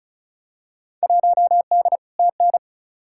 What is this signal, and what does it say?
Morse code, heard as a keyed tone, about 34 wpm, 700 Hz. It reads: 1D TN
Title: Field Day Exchanges - 35wpm